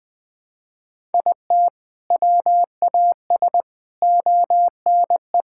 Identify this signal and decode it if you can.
Morse code, heard as a keyed tone, about 20 wpm, 700 Hz. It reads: IT WAS ONE
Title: Build A Phrase - 4-Word Phrases - 20wpm